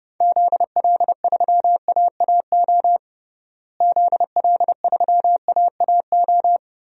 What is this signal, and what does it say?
Morse code, heard as a keyed tone, about 30 wpm, 700 Hz. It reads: ZL3AAO ZL3AAO
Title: DXCC-100 Call Signs - 6 Characters - Repeated 2x - 30wpm